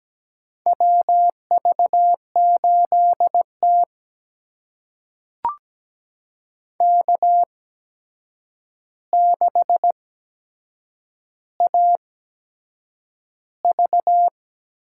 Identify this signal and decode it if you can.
Morse code, heard as a keyed tone, about 17 wpm, spaced as for 7 wpm, 700 Hz. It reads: WV8T K6AV
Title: USA Call Signs - 4 Characters - 8x Character Spacing - 17wpm